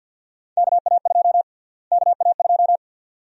Morse code, heard as a keyed tone, about 45 wpm, 700 Hz. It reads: KA2 KA2
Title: USA Call Sign Prefixes - Repeated 2x - 50wpm